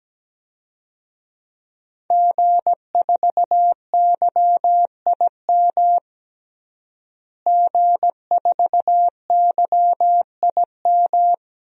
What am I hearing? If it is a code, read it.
Morse code, heard as a keyed tone, about 17 wpm, 700 Hz. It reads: G4YIM G4YIM
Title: DXCC-100 Call Signs - 5 Characters - Repeated 3x - 17wpm